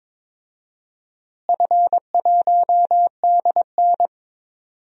Morse code, heard as a keyed tone, about 22 wpm, 700 Hz. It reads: F1DN
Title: DXCC-100 Call Signs - 4 Characters - Repeated 2x - 22wpm